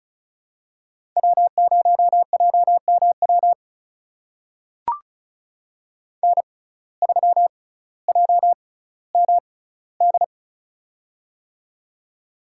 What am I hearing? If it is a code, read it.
Morse code, heard as a keyed tone, about 34 wpm, spaced as for 18 wpm, 700 Hz. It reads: W0JMW N3JMD
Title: USA Call Signs - 5 Characters - 6x Character Spacing - 35wpm